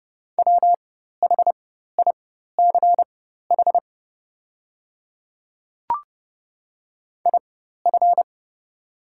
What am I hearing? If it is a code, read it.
Morse code, heard as a keyed tone, about 30 wpm, spaced as for 14 wpm, 690 Hz. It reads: WHICH IF
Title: 4x Character Spacing - Top 100 Words - 30wpm